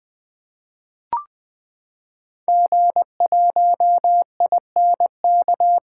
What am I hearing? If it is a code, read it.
Morse code, heard as a keyed tone, about 20 wpm, 700 Hz. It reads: G1INK
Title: 2023 SOTA Activator Call Signs - 20wpm